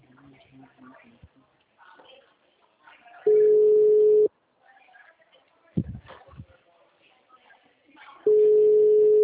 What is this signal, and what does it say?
Ringback tone (Europe, 425 Hz), with speech.
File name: out- -2003- - - .805